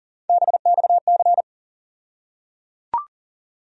Morse code, heard as a keyed tone, about 37 wpm, 700 Hz. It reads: BXC
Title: USA Call Sign Postfixes - 3 Letters 40wpm